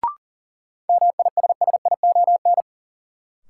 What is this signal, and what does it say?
Morse code, heard as a keyed tone, about 40 wpm, 700 Hz. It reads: MISSION
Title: Sentences from Top 1600 Words - New Word Drill Alt - 40wpm